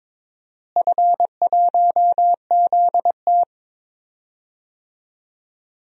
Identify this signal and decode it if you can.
Morse code, heard as a keyed tone, about 22 wpm, 700 Hz. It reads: F1ZT
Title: DXCC-100 Call Signs - 4 Characters - Repeated 2x - 22wpm